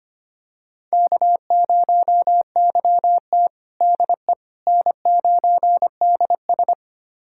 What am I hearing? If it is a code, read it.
Morse code, heard as a keyed tone, about 25 wpm, 700 Hz. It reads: K0YT DE N9DS
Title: USA Callsign DE Callsign - 4 Characters - 25wpm